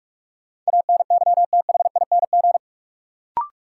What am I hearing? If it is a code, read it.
Morse code, heard as a keyed tone, about 42 wpm, 700 Hz. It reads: ANYTHING
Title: Sets of 4 Words - New Word Drill - 45wpm